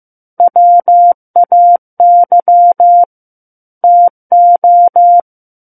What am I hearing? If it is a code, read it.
Morse code, heard as a keyed tone, about 15 wpm, 700 Hz. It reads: WAY TO
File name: Book One / 06-Chapter 6 - 15 WPM